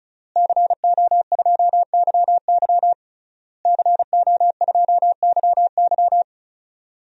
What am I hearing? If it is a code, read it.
Morse code, heard as a keyed tone, about 36 wpm, 700 Hz. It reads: CO2YY CO2YY
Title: DXCC-100 Call Signs - 5 Characters - Repeated 3x - 35wpm